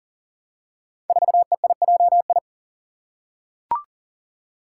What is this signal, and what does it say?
Morse code, heard as a keyed tone, about 40 wpm, 700 Hz. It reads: 4EIJI
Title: Five Challenging Letter-Number Combinations 40wpm